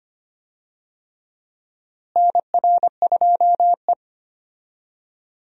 Morse code, heard as a keyed tone, about 25 wpm, 700 Hz. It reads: NR2E
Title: USA Call Signs - 4 Characters - Repeated 3x - 25wpm